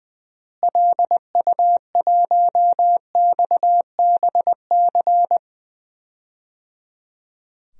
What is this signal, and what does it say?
Morse code, heard as a keyed tone, about 21 wpm, 700 Hz. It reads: LU1XBC
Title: DXCC-100 Call Signs - 6 Characters - Repeated 3x - 20wpm